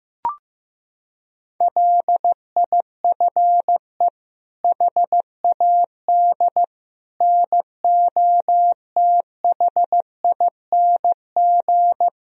Morse code, heard as a keyed tone, about 15 wpm, 700 Hz. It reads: LIFE HAD NOTHING
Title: Sentences from Top 1000 Words - Review Part 4 - 15wpm